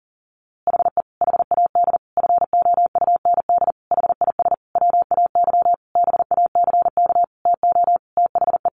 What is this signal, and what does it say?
Morse code, heard as a keyed tone, about 40 wpm, 700 Hz. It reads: HE HAD FOUND HIS WAY BACK TO THE
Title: A Sound Of Thunder 40wpm